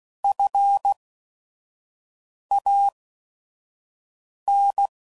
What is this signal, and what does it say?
Morse code, heard as a keyed tone, about 16 wpm, spaced as for 5 wpm, 800 Hz. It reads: FAN
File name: Prefixes / By First Letter / Fs